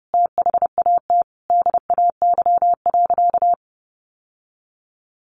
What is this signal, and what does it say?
Morse code, heard as a keyed tone, about 30 wpm, 700 Hz. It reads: THAT DAY.
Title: Sentences from Top 1300 Words 30wpm